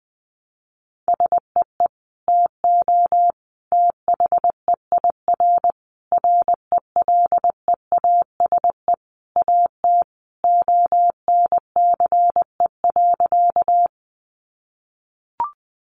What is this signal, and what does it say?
Morse code, heard as a keyed tone, about 20 wpm, 700 Hz. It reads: SEE TO THEIR RELEASE AT ONCE.
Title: Sentences from Top 1100 Words 20wpm